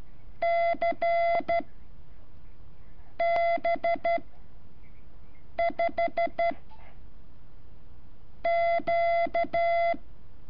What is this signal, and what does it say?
Morse code, heard as a keyed tone, about 13 wpm, spaced as for 5 wpm, 680 Hz. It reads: CB5Q
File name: curso / ejercicio 17